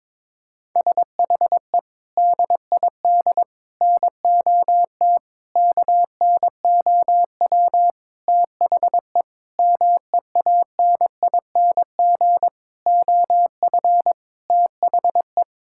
Morse code, heard as a keyed tone, about 22 wpm, 700 Hz. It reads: SHE DID NOT KNOW THE MEANING OF THE
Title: Sentences from Top 1000 Words - Review Part 6 - 22wpm